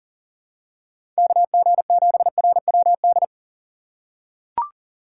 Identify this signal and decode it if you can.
Morse code, heard as a keyed tone, about 40 wpm, 700 Hz. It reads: KG7RWD
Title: USA Call Signs - 2x Character Spacing - 40wpm